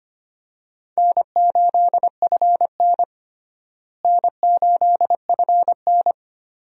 Morse code, heard as a keyed tone, about 25 wpm, 700 Hz. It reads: N8FN N8FN
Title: USA Call Signs - 4 Characters - Repeated 2x - 25wpm